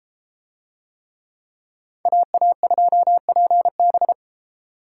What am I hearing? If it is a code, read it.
Morse code, heard as a keyed tone, about 31 wpm, 700 Hz. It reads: AA2PB